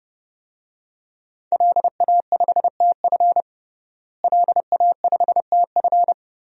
Morse code, heard as a keyed tone, about 30 wpm, 700 Hz. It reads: LA5TF LA5TF
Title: DXCC-100 Call Signs - 5 Characters - Repeated 2x - 30wpm